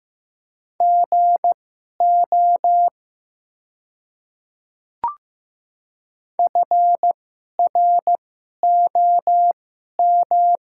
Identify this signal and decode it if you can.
Morse code, heard as a keyed tone, about 15 wpm, spaced as for 11 wpm, 700 Hz. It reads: GO FROM
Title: 2x Character Spacing - Top 100 Words - 15wpm